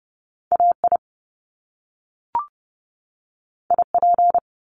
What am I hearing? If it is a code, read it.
Morse code, heard as a keyed tone, about 30 wpm, 700 Hz. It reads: AI IP
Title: Top Two Letter Combinations in Words 30wpm